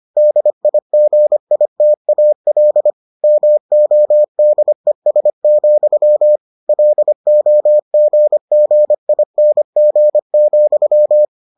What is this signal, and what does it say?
Morse code, heard as a keyed tone, about 25 wpm, 600 Hz. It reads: DIGITAL MODES, LOGGING,